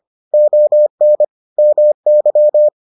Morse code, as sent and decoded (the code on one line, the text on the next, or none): --- -. / -- -.--
ON MY